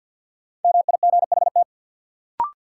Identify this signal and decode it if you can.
Morse code, heard as a keyed tone, about 53 wpm, 700 Hz. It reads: MIGHT
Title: Sets of 2 Words - New Word Drill - 50wpm